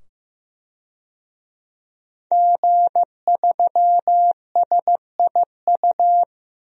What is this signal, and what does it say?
Morse code, heard as a keyed tone, about 15 wpm, 700 Hz. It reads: G3SIU
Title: DXCC-100 Call Signs - 5 Characters - 15wpm